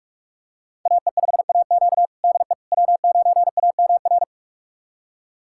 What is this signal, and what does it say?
Morse code, heard as a keyed tone, about 46 wpm, 700 Hz. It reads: AE5AQ DE W9AMR
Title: USA Callsign DE Callsign - 5 Characters - 45wpm